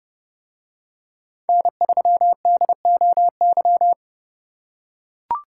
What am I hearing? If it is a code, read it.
Morse code, heard as a keyed tone, about 30 wpm, 700 Hz. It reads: N3DOY